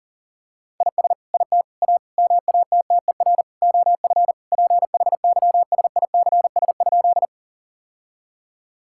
Morse code, heard as a keyed tone, about 40 wpm, 700 Hz. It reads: IS IT A MATTER OF PHYSICS?
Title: Sentences from Top 1000 Words - Review Part 5 - 40wpm